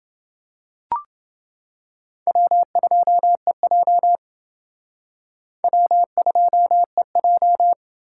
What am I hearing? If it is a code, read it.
Morse code, heard as a keyed tone, about 29 wpm, 700 Hz. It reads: W2EJ W2EJ